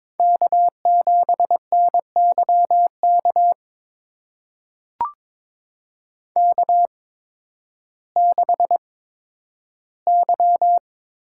Morse code, heard as a keyed tone, about 22 wpm, spaced as for 12 wpm, 700 Hz. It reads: K7NYK K6Y
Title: USA Call Signs - 5 Characters - 8x Character Spacing - 22wpm